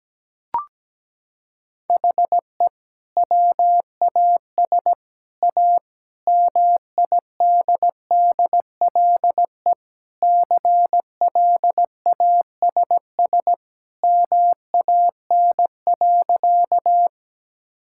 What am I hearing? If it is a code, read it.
Morse code, heard as a keyed tone, about 17 wpm, 700 Hz. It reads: HE WAS A MIDDLE CLASS MAN.
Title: Sentences from Top 1000 Words - Review Part 6 - 17wpm